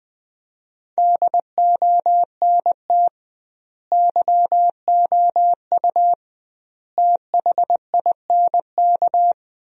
Morse code, heard as a keyed tone, about 20 wpm, 700 Hz. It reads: DONT YOU THINK
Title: 2x Word-Spacing - Sentences from Top 300 Words - 20wpm